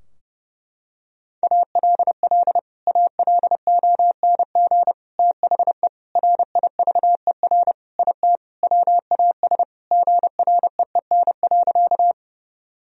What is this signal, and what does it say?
Morse code, heard as a keyed tone, about 30 wpm, 700 Hz. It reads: ALL ALONG THE RIVER IT WAS GREEN.
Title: Sentences from Top 1000 Words - Review Part 2 - 30wpm